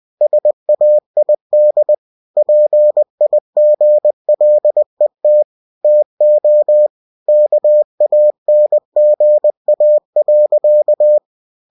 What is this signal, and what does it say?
Morse code, heard as a keyed tone, about 20 wpm, 600 Hz. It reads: SAID PIGLET TO KANGA.
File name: cw-books / winnie-the-pooh / winnie-20wpm-page-0081